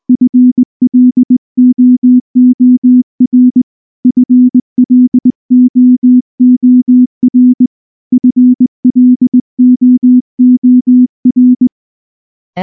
Morse code, heard as a keyed tone, about 19 wpm, 250 Hz. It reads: FLOOR FLOOR FLOOR
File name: morse / morse-training-20WPM-8